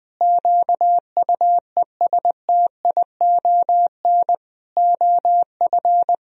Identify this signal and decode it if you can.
Morse code, heard as a keyed tone, about 20 wpm, 700 Hz. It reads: QUESTION OF
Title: Sentences from Top 1000 Words - Review Part 4 - 20wpm